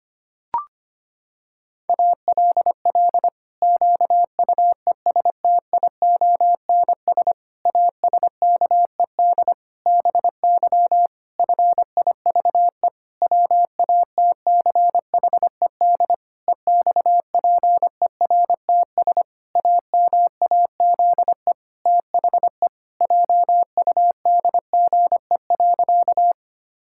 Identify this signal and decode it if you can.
Morse code, heard as a keyed tone, about 25 wpm, 700 Hz. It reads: ALL QUESTIONS ASKED BY FIVE WATCHED EXPERTS AMAZE THE JUDGE.
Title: Pangram Challenge v2 - 25wpm